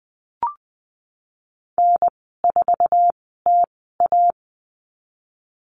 Morse code, heard as a keyed tone, about 20 wpm, spaced as for 15 wpm, 700 Hz. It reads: N4TA
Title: 055 Lesson 10g - Letter r - Call Signs 20wpm